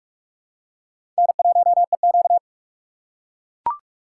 Morse code, heard as a keyed tone, about 44 wpm, 700 Hz. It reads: N1EQ
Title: USA Call Signs - 4 Characters - 6x Character Spacing - 45wpm